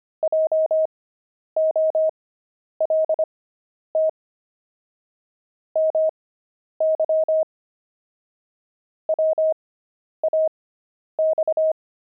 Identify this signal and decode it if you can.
Morse code, heard as a keyed tone, about 25 wpm, spaced as for 10 wpm, 630 Hz. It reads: JOLT MY WAX